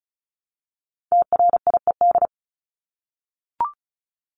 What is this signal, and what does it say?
Morse code, heard as a keyed tone, about 34 wpm, 700 Hz. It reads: TRIED
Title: ED Suffix - Top 100 Words with Base Words 35wpm